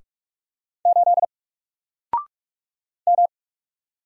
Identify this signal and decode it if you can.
Morse code, heard as a keyed tone, about 44 wpm, 700 Hz. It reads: 8 M
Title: Single Letter-Number 45wpm